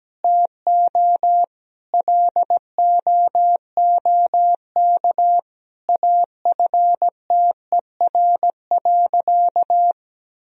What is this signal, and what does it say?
Morse code, heard as a keyed tone, about 17 wpm, 700 Hz. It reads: TO LOOK AFTER.
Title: Sentences from Top 1000 Words - Review Part 3 - 17wpm